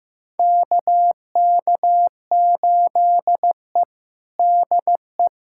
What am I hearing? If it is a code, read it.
Morse code, heard as a keyed tone, about 15 wpm, 700 Hz. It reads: KK8E DE
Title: USA Callsign DE Callsign - 4 Characters - 15wpm